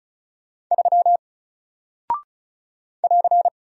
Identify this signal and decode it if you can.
Morse code, heard as a keyed tone, about 35 wpm, 700 Hz. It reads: 3 <AR>
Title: Single Character (Letter:Number:Punctuation:Prosign) Alt 35wpm